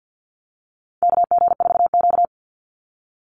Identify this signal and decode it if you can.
Morse code, heard as a keyed tone, about 50 wpm, 700 Hz. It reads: KG4Q
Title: Call Signs - 50wpm